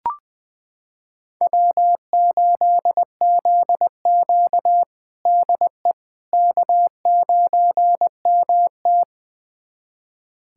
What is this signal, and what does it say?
Morse code, heard as a keyed tone, about 21 wpm, 700 Hz. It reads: W8ZQ DE K9MT
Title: USA Callsign DE Callsign - 4 Characters - 20wpm